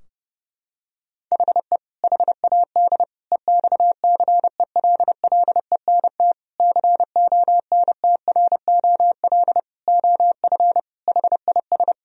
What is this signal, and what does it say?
Morse code, heard as a keyed tone, about 31 wpm, 700 Hz. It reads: HE HAD EXCELLENT CONTROL OF HIS
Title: Sentences from Top 1000 Words - Review Part 3 - 30wpm